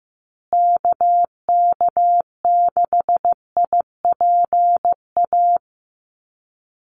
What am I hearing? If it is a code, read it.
Morse code, heard as a keyed tone, about 15 wpm, 700 Hz. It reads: KK6IPA
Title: USA Call Signs Repeated 2x 15wpm